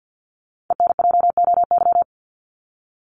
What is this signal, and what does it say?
Morse code, heard as a keyed tone, about 54 wpm, 700 Hz. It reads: ENJOY
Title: Top 751-1000 Words 50wpm